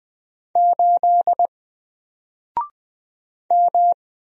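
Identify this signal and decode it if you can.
Morse code, heard as a keyed tone, about 20 wpm, 700 Hz. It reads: 8 M
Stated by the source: Single letter and number Morse drill at 20 wpm, from N4LCD.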